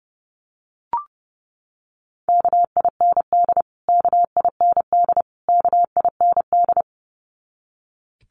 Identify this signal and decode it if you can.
Morse code, heard as a keyed tone, about 30 wpm, 700 Hz. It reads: KIND KIND KIND
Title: Top 300 Words Repeated 30wpm